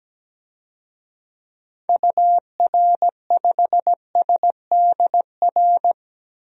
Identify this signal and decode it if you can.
Morse code, heard as a keyed tone, about 17 wpm, 700 Hz. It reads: UR5SDR
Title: DXCC-100 Call Signs - 6 Characters - Repeated 3x - 17wpm